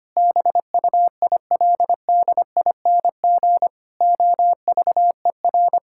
Morse code, heard as a keyed tone, about 25 wpm, 700 Hz. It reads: BUILDING OVER
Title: Sentences from Top 1000 Words - Review Part 5 - 25wpm